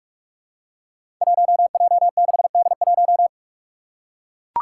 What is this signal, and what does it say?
Morse code, heard as a keyed tone, about 44 wpm, 700 Hz. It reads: 1J6D1
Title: Five Challenging Letter-Number Combinations 45wpm